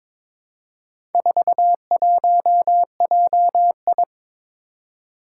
Morse code, heard as a keyed tone, about 22 wpm, 700 Hz. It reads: 41JI
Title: Four Challenging Letter-Number Combinations 22wpm